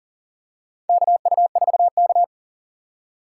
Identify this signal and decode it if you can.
Morse code, heard as a keyed tone, about 40 wpm, 700 Hz. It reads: KU4K
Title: USA Call Signs - 4 Characters - 4x Character Spacing - 40wpm